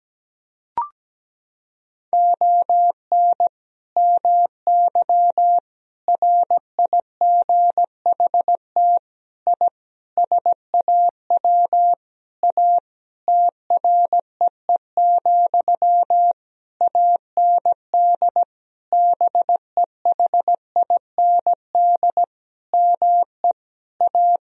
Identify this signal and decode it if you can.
Morse code, heard as a keyed tone, about 17 wpm, 700 Hz. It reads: ON MY RIGHT I SAW A TREE, AND BEHIND ME A
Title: Sentences from Top 1000 Words - Review Part 2 - 17wpm